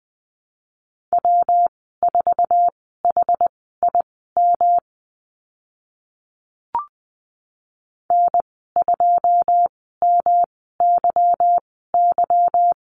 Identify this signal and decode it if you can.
Morse code, heard as a keyed tone, about 20 wpm, spaced as for 14 wpm, 700 Hz. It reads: W4HIM N2MYY
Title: 154 Lesson 25g - Letter p - Call Signs 20wpm